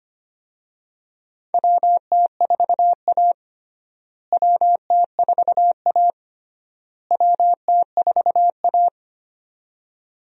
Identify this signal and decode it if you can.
Morse code, heard as a keyed tone, about 25 wpm, 700 Hz. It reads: WT4A WT4A WT4A
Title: USA Call Signs - 4 Characters - Repeated 3x - 25wpm